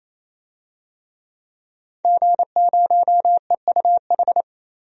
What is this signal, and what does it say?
Morse code, heard as a keyed tone, about 28 wpm, 700 Hz. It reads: G0EUH
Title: DXCC-100 Call Signs - 5 Characters - 28wpm